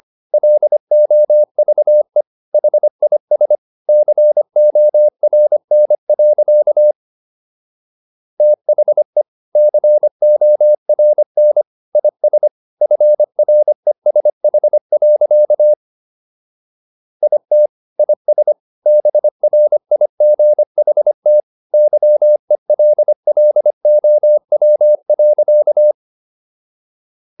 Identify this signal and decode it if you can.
Morse code, heard as a keyed tone, about 25 wpm, 600 Hz. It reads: LOVE HIS CORN. THE CORN IS FRESH. IT IS BRIGHT YELLOW.
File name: SS111 25